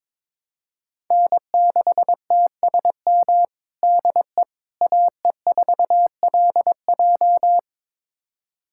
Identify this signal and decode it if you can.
Morse code, heard as a keyed tone, about 22 wpm, 700 Hz. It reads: N6TSM DE AE4LJ
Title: USA Callsign DE Callsign - 5 Characters - 22wpm